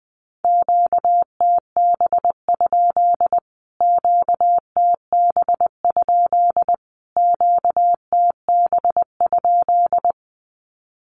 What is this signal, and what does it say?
Morse code, heard as a keyed tone, about 20 wpm, 700 Hz. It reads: QTB? QTB? QTB?
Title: Q-Codes Repeated 3x 20wpm